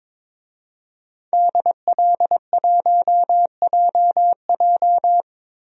Morse code, heard as a keyed tone, about 22 wpm, 700 Hz. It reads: DL1JJ
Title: DXCC-100 Call Signs - 5 Characters - 22wpm